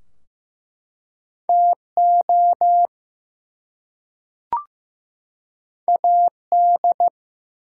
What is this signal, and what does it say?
Morse code, heard as a keyed tone, about 15 wpm, 700 Hz. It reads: TO AD
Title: Two Letter Words - 15wpm